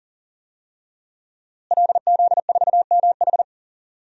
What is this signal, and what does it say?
Morse code, heard as a keyed tone, about 37 wpm, 700 Hz. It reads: LZ4MH
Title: DXCC-100 Call Signs - 5 Characters - Repeated 3x - 40wpm